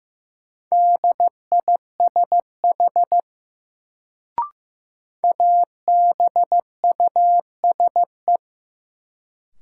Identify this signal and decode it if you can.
Morse code, heard as a keyed tone, about 15 wpm, 700 Hz. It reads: DISH ABUSE
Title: Sentences from Top 1300 Words - New Word Drill Alt - 15wpm